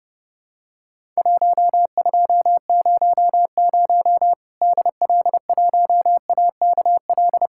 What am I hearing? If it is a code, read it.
Morse code, heard as a keyed tone, about 30 wpm, 700 Hz. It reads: 1200 DL1AKL
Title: Worked All Europe DX Contest - QTC Exchange - 30wpm